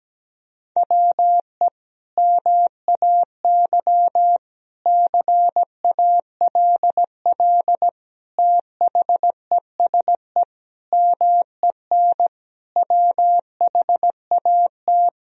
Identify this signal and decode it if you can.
Morse code, heard as a keyed tone, about 17 wpm, 700 Hz. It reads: WE MAY CALL THESE MEN WHAT
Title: Sentences from Top 1000 Words - Review Part 1 - 17wpm